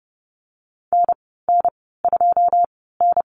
Morse code, heard as a keyed tone, about 30 wpm, spaced as for 17 wpm, 700 Hz. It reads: NN2N
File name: Lesson 14g - Number 2 - Call Signs 30wpm